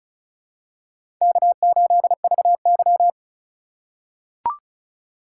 Morse code, heard as a keyed tone, about 35 wpm, 700 Hz. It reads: K8VY
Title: USA Call Signs - 4 Characters - 4x Character Spacing - 35wpm